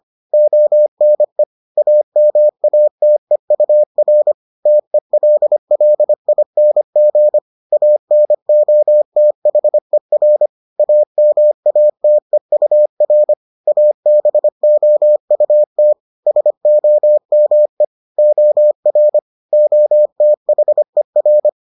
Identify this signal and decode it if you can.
Morse code, heard as a keyed tone, about 25 wpm, 600 Hz. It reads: ONE AMATEUR TELLING ANOTHER AMATEUR ABOUT SOME OR OTHER